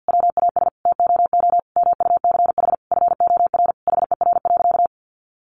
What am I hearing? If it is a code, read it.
Morse code, heard as a keyed tone, about 50 wpm, 700 Hz. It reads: WAS TOO MUCH FOR HER.
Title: Sentences from Top 1200 Words 50wpm